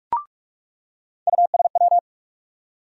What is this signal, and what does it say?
Morse code, heard as a keyed tone, about 42 wpm, 700 Hz. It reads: USW
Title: USA Call Sign Postfixes - 3 Letters 45wpm